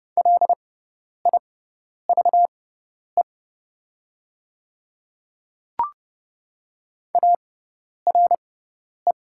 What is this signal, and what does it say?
Morse code, heard as a keyed tone, about 30 wpm, spaced as for 10 wpm, 700 Hz. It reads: LIVE ARE
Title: 6x Character Spacing - Top 200 Words - 30wpm